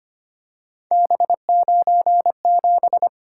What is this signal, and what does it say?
Morse code, heard as a keyed tone, about 25 wpm, 700 Hz. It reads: B97
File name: Three Letter-Number Combinations - 25wpm